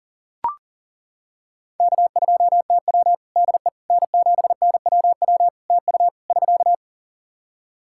Morse code, heard as a keyed tone, about 40 wpm, 700 Hz. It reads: K2TW DE N7NWW TU <SK>
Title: K1USN SST QSO Practice 40wpm